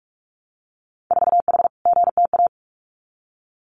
Morse code, heard as a keyed tone, about 46 wpm, 700 Hz. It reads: 4H GTA
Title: Winter Field Day Exchanges Encore 45wpm